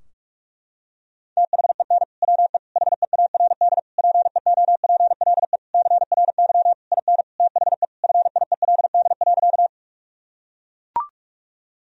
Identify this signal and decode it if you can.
Morse code, heard as a keyed tone, about 42 wpm, 700 Hz. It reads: THEN WE HEARD PEOPLE CRY IN THE FIELD.
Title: Sentences from Top 1000 Words - Review Part 2 - 45wpm